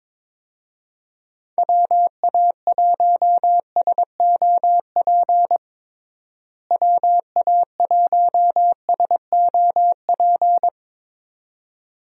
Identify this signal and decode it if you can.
Morse code, heard as a keyed tone, about 22 wpm, 700 Hz. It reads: WA1SOP WA1SOP
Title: USA Call Signs - 6 Characters - Repeated 2x - 22wpm